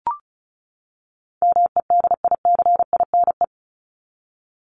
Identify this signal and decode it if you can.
Morse code, heard as a keyed tone, about 34 wpm, 700 Hz. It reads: MEDICINE
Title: Top 751-1000 Words 35wpm